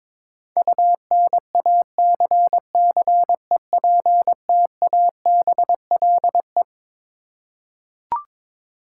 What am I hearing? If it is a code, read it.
Morse code, heard as a keyed tone, about 22 wpm, 700 Hz. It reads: UNACCEPTABLE
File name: Twelve Letter Words - 22wpm